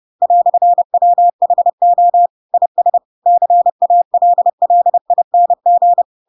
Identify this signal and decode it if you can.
Morse code, heard as a keyed tone, about 30 wpm, 700 Hz. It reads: "WHO IS CALLING
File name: Pinocchio / Pinocchio ch4 30 30 ik1hge